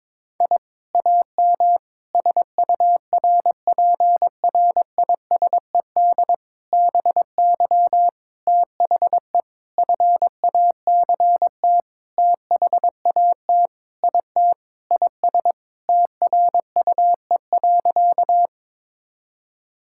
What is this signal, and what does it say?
Morse code, heard as a keyed tone, about 21 wpm, 700 Hz. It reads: I AM SURPRISED BY THE FACT THAT IT IS TRUE.
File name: practice / 22wpm / Sentences with 4-Word Phrases - 22wpm